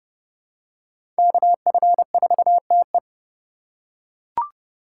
Morse code, heard as a keyed tone, about 30 wpm, 700 Hz. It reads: KF4TE
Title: USA Call Signs - 5 Characters - 4x Character Spacing - 30wpm